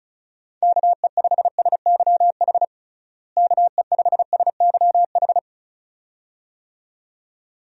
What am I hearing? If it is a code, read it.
Morse code, heard as a keyed tone, about 36 wpm, 700 Hz. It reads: KE5SYH KE5SYH
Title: USA Call Signs - 6 Characters - Repeated 2x - 35wpm